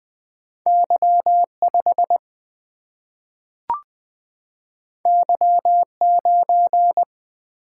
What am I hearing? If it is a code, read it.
Morse code, heard as a keyed tone, about 20 wpm, 700 Hz. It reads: Y5 Y9
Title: Single-Letter Single-Number 20wpm